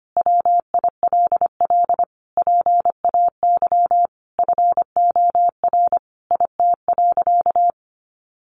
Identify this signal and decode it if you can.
Morse code, heard as a keyed tone, about 25 wpm, 700 Hz. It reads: WILL PAY FOR IT.
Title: Sentences from Top 1300 Words 25wpm